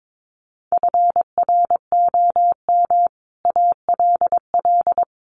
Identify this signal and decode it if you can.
Morse code, heard as a keyed tone, about 22 wpm, 700 Hz. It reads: FROM ALL
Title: Sets of 5 Words 22wpm